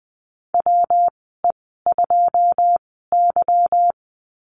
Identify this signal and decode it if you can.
Morse code, heard as a keyed tone, about 20 wpm, spaced as for 15 wpm, 700 Hz. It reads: WE2Y
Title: 193 Lesson 31g - Letter v - Call Signs 20wpm